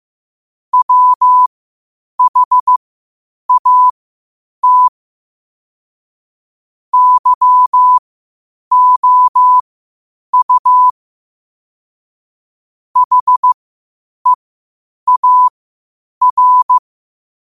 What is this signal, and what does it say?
Morse code, heard as a keyed tone, about 15 wpm, spaced as for 8 wpm, 1000 Hz. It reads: WHAT YOU HEAR
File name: qotdfiles / 8wpm